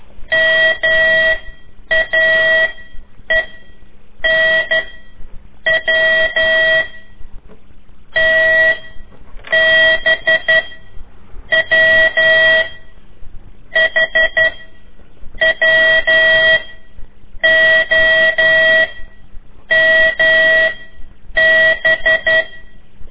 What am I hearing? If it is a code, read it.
Morse code, heard as a keyed tone, about 12 wpm, 1950 Hz. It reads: M A E N W T B W H W O M B
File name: ik1weg2 / CORSI CW / lez4-3